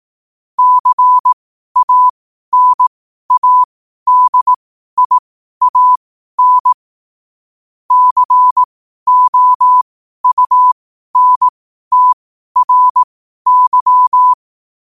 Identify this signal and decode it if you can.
Morse code, heard as a keyed tone, about 18 wpm, spaced as for 12 wpm, 1000 Hz. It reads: CANADIAN COUNTRY
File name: qotdfiles / 12wpm